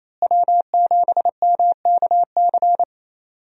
Morse code, heard as a keyed tone, about 28 wpm, 700 Hz. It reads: W7MKC
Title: USA Call Signs - 5 Characters - Repeated 3x - 28wpm